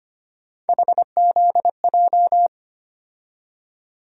Morse code, heard as a keyed tone, about 25 wpm, 700 Hz. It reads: HZJ
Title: USA Call Sign Postfixes - 3 Letters 25wpm